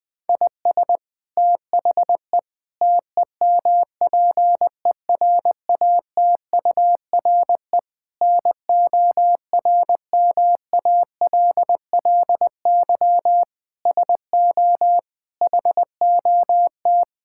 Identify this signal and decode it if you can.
Morse code, heard as a keyed tone, about 20 wpm, 700 Hz. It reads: IS THE TEMPERATURE NORMALLY SO HOT
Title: Sentences from Top 1000 Words - Review Part 6 - 20wpm